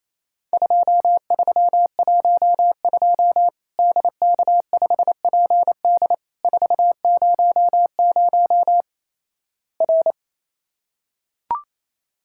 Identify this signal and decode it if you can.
Morse code, heard as a keyed tone, about 28 wpm, 700 Hz. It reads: 2312 DK5PD 400 R
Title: Worked All Europe DX Contest - QTC Exchange - 28wpm